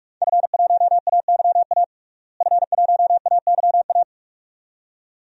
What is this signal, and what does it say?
Morse code, heard as a keyed tone, about 44 wpm, 700 Hz. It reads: F1AYA F1AYA
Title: DXCC-100 Call Signs - 5 Characters - Repeated 2x - 45wpm